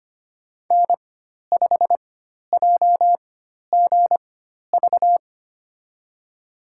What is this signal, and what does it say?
Morse code, heard as a keyed tone, about 25 wpm, spaced as for 12 wpm, 700 Hz. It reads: N5JGV
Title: USA Call Signs - 5 Characters - 4x Character Spacing - 25wpm